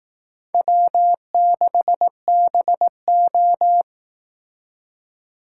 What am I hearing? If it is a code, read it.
Morse code, heard as a keyed tone, about 18 wpm, 700 Hz. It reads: W6BO
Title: Speed Racing - USA 1x2 Call Signs 15wpm